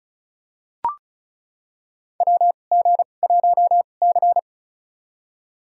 Morse code, heard as a keyed tone, about 36 wpm, spaced as for 25 wpm, 700 Hz. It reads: WG1C
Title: USA Call Signs - 2x Character Spacing - 35wpm